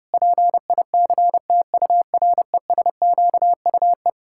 Morse code, heard as a keyed tone, about 30 wpm, 700 Hz. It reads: PICTURESQUE
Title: Pangram Challenge v2 - 30wpm